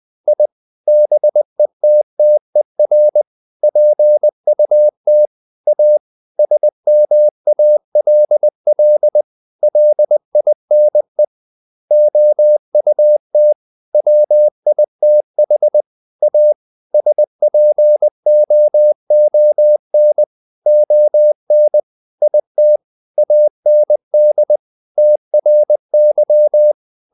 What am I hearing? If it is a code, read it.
Morse code, heard as a keyed tone, about 20 wpm, 600 Hz. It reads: I BETTER PUT A SMALL LINE OUT WITH A SPOON ON IT AND TRY
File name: the old man and the sea part 2 20wpm